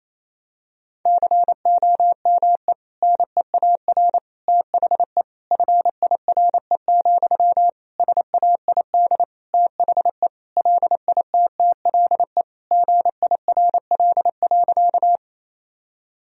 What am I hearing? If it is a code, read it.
Morse code, heard as a keyed tone, about 28 wpm, 700 Hz. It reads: COME NEAR THE FIRE, SAID THE LITTLE GIRL.
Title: Sentences from Top 1000 Words - Review Part 1 - 28wpm